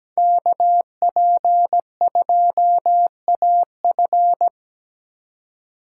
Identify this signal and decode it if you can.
Morse code, heard as a keyed tone, about 17 wpm, 700 Hz. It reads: KP2AF
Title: USA Call Signs - 5 Characters - 2x Character Spacing - 17wpm